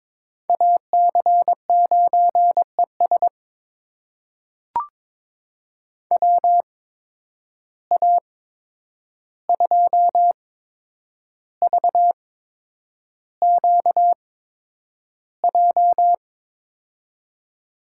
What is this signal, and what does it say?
Morse code, heard as a keyed tone, about 22 wpm, spaced as for 9 wpm, 700 Hz. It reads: AC9ES WA2VQJ
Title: USA Call Signs - 8x Character Spacing - 22wpm